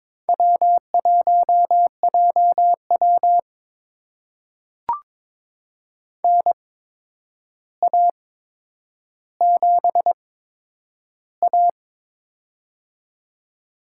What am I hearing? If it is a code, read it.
Morse code, heard as a keyed tone, about 22 wpm, spaced as for 9 wpm, 700 Hz. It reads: W1JW NA7A
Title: USA Call Signs - 4 Characters - 8x Character Spacing - 22wpm